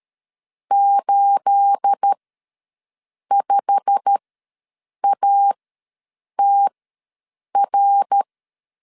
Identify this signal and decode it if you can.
Morse code, heard as a keyed tone, about 13 wpm, spaced as for 7 wpm, 800 Hz. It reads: 85ATR
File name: LEZIONI / 9 286 ws 12wpm 3351